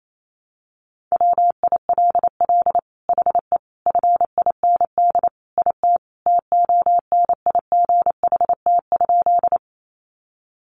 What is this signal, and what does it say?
Morse code, heard as a keyed tone, about 28 wpm, 700 Hz. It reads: WILL HE FIND IT TONIGHT?
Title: Sentences from Top 600 Words 28wpm